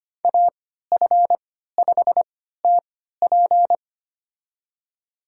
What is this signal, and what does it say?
Morse code, heard as a keyed tone, about 25 wpm, spaced as for 14 wpm, 700 Hz. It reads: AF5TP
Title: USA Call Signs - 3x Character Spacing - 25wpm